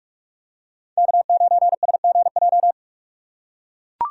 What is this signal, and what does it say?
Morse code, heard as a keyed tone, about 44 wpm, 700 Hz. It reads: K9SGJ